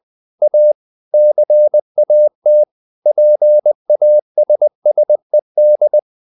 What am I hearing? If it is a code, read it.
Morse code, heard as a keyed tone, about 20 wpm, 600 Hz. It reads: A CAT PASSED
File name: the old man and the sea part 2 20wpm